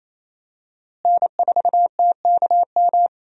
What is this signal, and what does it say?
Morse code, heard as a keyed tone, about 28 wpm, 700 Hz. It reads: N4TKM
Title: USA Call Signs - 5 Characters - Repeated 3x - 28wpm